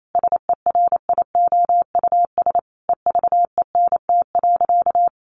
Morse code, heard as a keyed tone, about 28 wpm, 700 Hz. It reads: SERIOUS EVENT.
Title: Sentences from Top 1400 Words 28wpm